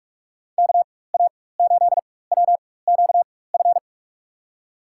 Morse code, heard as a keyed tone, about 42 wpm, spaced as for 20 wpm, 700 Hz. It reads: KA8WQF
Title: USA Call Signs - 6 Characters - 4x Character Spacing - 45wpm